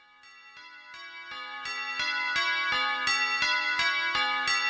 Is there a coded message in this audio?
Yes: Emergency Alert System (SAME) tones from 0.5 to 4.1 s, heard as harsh buzzing bursts.